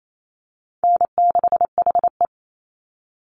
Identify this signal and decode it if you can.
Morse code, heard as a keyed tone, about 28 wpm, 700 Hz. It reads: N6HE
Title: Easy Call Signs - 28wpm